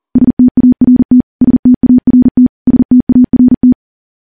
Morse code, heard as a keyed tone, about 40 wpm, 250 Hz. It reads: START START START
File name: morse / morse-training-40WPM